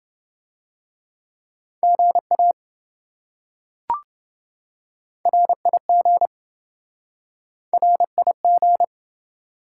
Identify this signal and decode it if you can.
Morse code, heard as a keyed tone, about 26 wpm, 700 Hz. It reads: GA RIG RIG